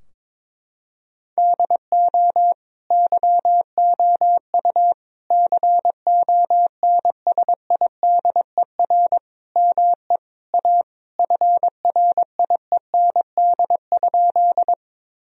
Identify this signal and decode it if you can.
Morse code, heard as a keyed tone, about 22 wpm, 700 Hz. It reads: DO YOU CONSIDER ME A FRIEND?
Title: Sentences from Top 1000 Words - Review Part 4 - 22wpm